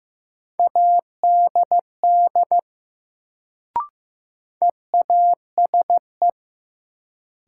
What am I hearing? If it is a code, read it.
Morse code, heard as a keyed tone, about 15 wpm, 700 Hz. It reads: ADD EASE
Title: Sentences from Top 1000 Words - Review Part 1 - New Word Drill - 15wpm